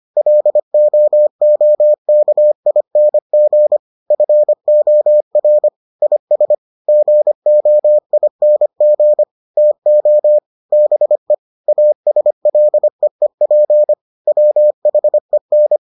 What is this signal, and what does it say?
Morse code, heard as a keyed tone, about 25 wpm, 600 Hz. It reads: LOOKING FOR IS GOING TO BE ASLEEP WHEN